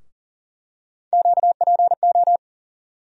Morse code, heard as a keyed tone, about 40 wpm, 700 Hz. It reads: QPO
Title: USA Call Sign Postfixes - 3 Letters 40wpm